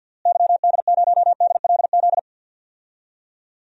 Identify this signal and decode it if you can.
Morse code, heard as a keyed tone, about 53 wpm, 700 Hz. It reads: YD0DLZ
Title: DXCC-100 Call Signs - 6 Characters - Repeated 3x - 50wpm